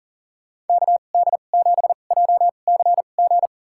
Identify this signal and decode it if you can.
Morse code, heard as a keyed tone, about 40 wpm, spaced as for 29 wpm, 700 Hz. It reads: KD7JCG
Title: USA Call Signs - 2x Character Spacing - 40wpm